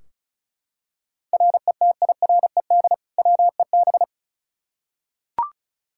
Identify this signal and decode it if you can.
Morse code, heard as a keyed tone, about 34 wpm, 700 Hz. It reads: RETIRED WEB
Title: K1USN SST QSO Practice 35wpm